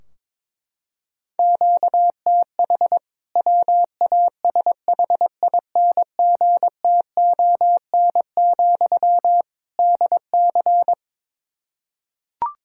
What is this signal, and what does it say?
Morse code, heard as a keyed tone, about 22 wpm, 700 Hz. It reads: QTH WASHINGTON, DC
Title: K1USN SST QSO Practice 22wpm